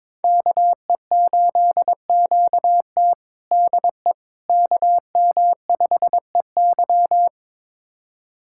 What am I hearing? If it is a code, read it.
Morse code, heard as a keyed tone, about 21 wpm, 700 Hz. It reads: KE8QT DE KM5EY
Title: USA Callsign DE Callsign - 5 Characters - 22wpm